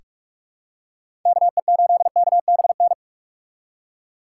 Morse code, heard as a keyed tone, about 46 wpm, 700 Hz. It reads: KE8KBN
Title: USA Call Signs - 6 Characters - 2x Character Spacing - 45wpm